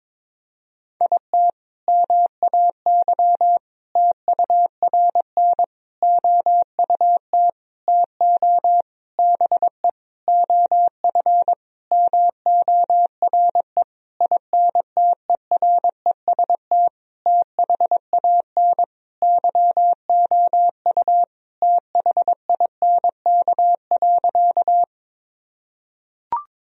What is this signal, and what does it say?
Morse code, heard as a keyed tone, about 22 wpm, 700 Hz. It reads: IT MAY TURN OUT TO BE OF MORE INTEREST THAN YOU THINK.
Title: Sentences from Top 1000 Words - Review Part 2 - 22wpm